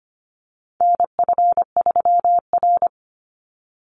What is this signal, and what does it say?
Morse code, heard as a keyed tone, about 25 wpm, 700 Hz. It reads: NF3R
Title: Call Signs - 25wpm